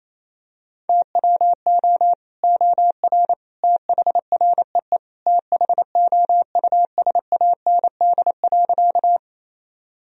Morse code, heard as a keyed tone, about 28 wpm, 700 Hz. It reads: TWO OR THREE THOUSAND.
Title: Sentences from Top 1000 Words - Review Part 3 - 28wpm